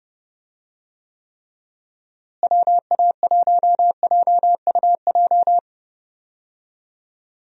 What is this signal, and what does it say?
Morse code, heard as a keyed tone, about 32 wpm, 700 Hz. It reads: WA1JUJ
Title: USA Call Signs - 6 Characters - Repeated 3x - 30wpm